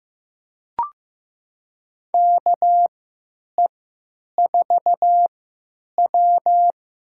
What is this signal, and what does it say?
Morse code, heard as a keyed tone, about 15 wpm, spaced as for 9 wpm, 700 Hz. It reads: KE4W
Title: USA Call Signs - 3x Character Spacing - 15wpm